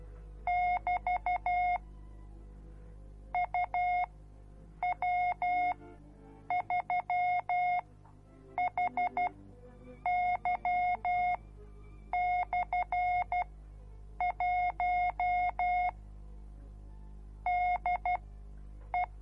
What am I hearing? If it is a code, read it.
Morse code, heard as a keyed tone, about 12 wpm, spaced as for 7 wpm, 720 Hz. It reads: <BT> UW3HY/1 DE